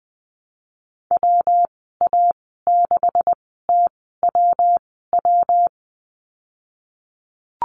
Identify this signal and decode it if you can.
Morse code, heard as a keyed tone, about 20 wpm, spaced as for 15 wpm, 700 Hz. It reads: WA6TWW
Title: 207 Lesson 33g - Letter j - Call Signs 20wpm